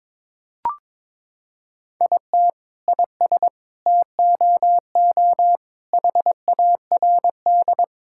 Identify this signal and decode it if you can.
Morse code, heard as a keyed tone, about 22 wpm, 700 Hz. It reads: IT IS TOO HARD
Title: Sentences from Top 1000 Words - Review Part 5 - 22wpm